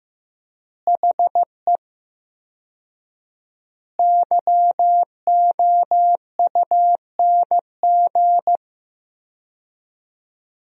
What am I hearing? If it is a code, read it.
Morse code, heard as a keyed tone, about 16 wpm, 700 Hz. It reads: HE YOUNG
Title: 4x Word-Spacing - Sentences from Top 400 Words - 15wpm